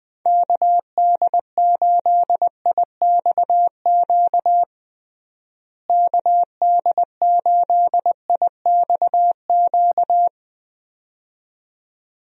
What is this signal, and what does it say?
Morse code, heard as a keyed tone, about 21 wpm, 700 Hz. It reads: KD8IXQ KD8IXQ
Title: USA Call Signs - 6 Characters - Repeated 2x - 20wpm